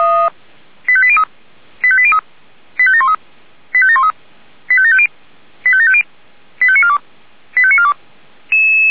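Five-tone selective call (ZVEI), heard as quick runs of five tones, tones of about 70 ms. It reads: 85803 85803 85713 85713 85750 85750 85942 85942
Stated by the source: ZVEI five-tone selective call received on an SDR.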